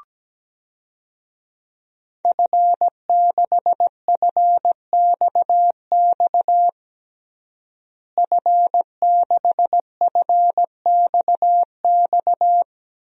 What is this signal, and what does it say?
Morse code, heard as a keyed tone, about 17 wpm, 700 Hz. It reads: F6FXX F6FXX
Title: DXCC-100 Call Signs - 5 Characters - Repeated 3x - 17wpm